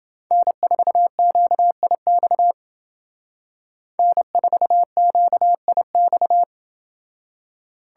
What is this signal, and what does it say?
Morse code, heard as a keyed tone, about 30 wpm, 700 Hz. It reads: N4QIX N4QIX